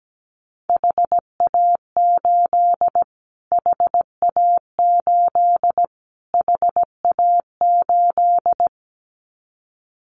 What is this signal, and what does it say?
Morse code, heard as a keyed tone, about 17 wpm, 700 Hz. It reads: HA8 HA8 HA8
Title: Callsign Prefixes for Top 25 DXCC - Repeated 3x - 17wpm